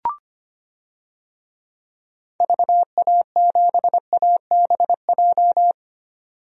Morse code, heard as a keyed tone, about 25 wpm, 700 Hz. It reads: VA7ABJ
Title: DXCC-100 Call Signs - 6 Characters - 25wpm